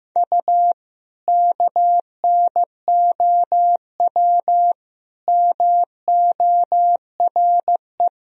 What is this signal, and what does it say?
Morse code, heard as a keyed tone, about 16 wpm, 700 Hz. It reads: U KNOW MORE
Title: Sentences from Top 100 Words and Abbreviations 15wpm